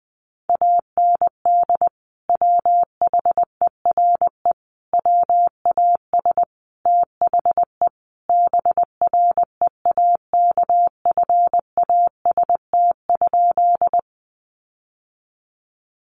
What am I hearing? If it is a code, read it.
Morse code, heard as a keyed tone, about 20 wpm, 700 Hz. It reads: AND WHERE WAS THE BREAKFAST?
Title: Sentences from Top 1500 Words 20wpm